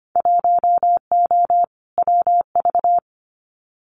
Morse code, heard as a keyed tone, about 25 wpm, 700 Hz. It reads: 1O WV
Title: Winter Field Day Exchanges Encore 25wpm